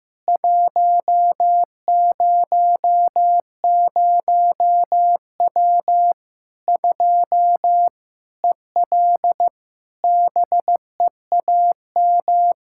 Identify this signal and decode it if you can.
Morse code, heard as a keyed tone, about 15 wpm, 700 Hz. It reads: 100W 2 EL BEAM
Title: Practice QSO - Exchange 2 - Info - 15wpm